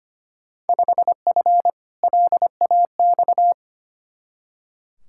Morse code, heard as a keyed tone, about 25 wpm, 700 Hz. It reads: 5F LAX
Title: Field Day Exchanges Encore - 25wpm